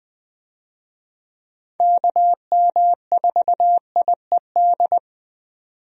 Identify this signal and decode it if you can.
Morse code, heard as a keyed tone, about 21 wpm, 700 Hz. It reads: KM4IED
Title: USA Call Signs - 6 Characters - 20wpm